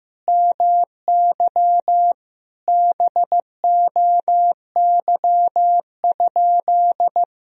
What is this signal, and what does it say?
Morse code, heard as a keyed tone, about 15 wpm, 700 Hz. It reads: MY BOY?
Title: Sentences from Top 1000 Words - Review Part 4 - 15wpm